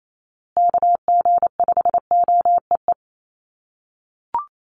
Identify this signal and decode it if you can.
Morse code, heard as a keyed tone, about 28 wpm, 700 Hz. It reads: KG5OEE
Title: USA Call Signs Repeated 2x 28wpm